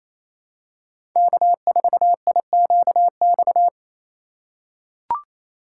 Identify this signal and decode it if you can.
Morse code, heard as a keyed tone, about 28 wpm, 700 Hz. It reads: K4IQX
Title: USA Call Signs - 5 Characters - 4x Character Spacing - 28wpm